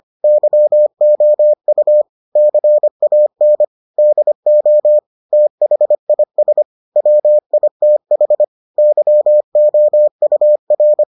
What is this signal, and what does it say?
Morse code, heard as a keyed tone, about 25 wpm, 600 Hz. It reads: YOU CAN DO THIS WITH YOUR